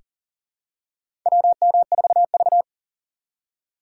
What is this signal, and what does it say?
Morse code, heard as a keyed tone, about 37 wpm, 700 Hz. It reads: WM4V
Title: USA Call Signs - 4 Characters - 6x Character Spacing - 40wpm